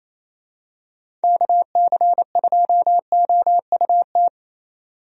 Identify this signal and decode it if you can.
Morse code, heard as a keyed tone, about 27 wpm, 700 Hz. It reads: KC2OUT
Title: USA Call Signs - 6 Characters - 28wpm